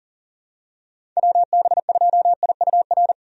Morse code, heard as a keyed tone, about 40 wpm, 700 Hz. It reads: WB2IUR
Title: USA Call Signs - 6 Characters - 40wpm